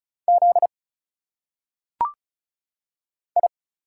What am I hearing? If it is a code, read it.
Morse code, heard as a keyed tone, about 34 wpm, 700 Hz. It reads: Z I
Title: Single Letters - Phonetic Pronunciation 35wpm